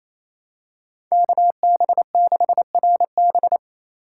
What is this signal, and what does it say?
Morse code, heard as a keyed tone, about 28 wpm, 700 Hz. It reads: KB6RB
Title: USA Call Signs - 5 Characters - 8x Character Spacing - 28wpm